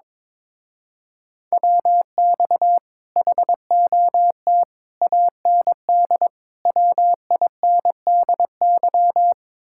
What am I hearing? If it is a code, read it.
Morse code, heard as a keyed tone, about 23 wpm, 700 Hz. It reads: WX HOT AND WINDY
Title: Speed-Racing Common QSO Phrases - 22wpm